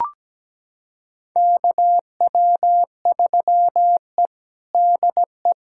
Morse code, heard as a keyed tone, about 17 wpm, 700 Hz. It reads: KW3E DE
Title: USA Callsign DE Callsign - 4 Characters - 17wpm